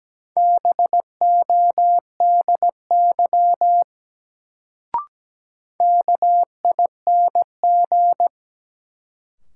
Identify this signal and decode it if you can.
Morse code, heard as a keyed tone, about 17 wpm, 700 Hz. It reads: BODY KING
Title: Sentences from Top 400 Words - New Word Drill - 17wpm